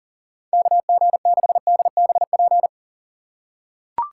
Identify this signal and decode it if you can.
Morse code, heard as a keyed tone, about 39 wpm, 700 Hz. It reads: KG6DBP
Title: USA Call Signs - 2x Character Spacing - 40wpm